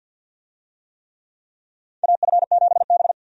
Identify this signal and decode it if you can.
Morse code, heard as a keyed tone, about 52 wpm, 700 Hz. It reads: AF7B
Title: USA Call Signs - 4 Characters - 50wpm